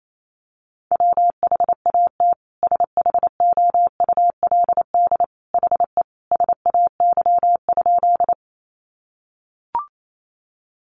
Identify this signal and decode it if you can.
Morse code, heard as a keyed tone, about 28 wpm, 700 Hz. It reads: WHAT SHOULD HE SAY?
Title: Sentences from Top 300 Words 28wpm